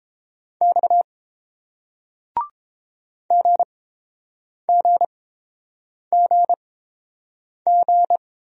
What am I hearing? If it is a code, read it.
Morse code, heard as a keyed tone, about 30 wpm, spaced as for 7 wpm, 690 Hz. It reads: X GGGG